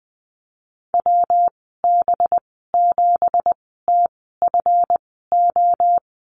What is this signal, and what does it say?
Morse code, heard as a keyed tone, about 20 wpm, spaced as for 14 wpm, 700 Hz. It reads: WB7TFO
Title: 193 Lesson 31g - Letter v - Call Signs 20wpm